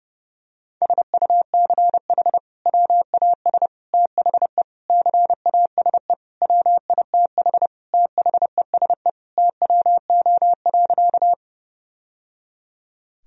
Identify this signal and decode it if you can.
Morse code, heard as a keyed tone, about 30 wpm, 700 Hz. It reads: SUCH WAS THE CASE WITH THESE TWO.
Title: Sentences from Top 1000 Words - Review Part 1 - 30wpm